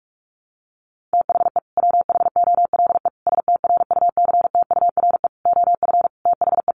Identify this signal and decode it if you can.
Morse code, heard as a keyed tone, about 44 wpm, 700 Hz. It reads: THE WHOLE STRUCTURE OF THE
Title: Sentences from Top 1500 Words 45wpm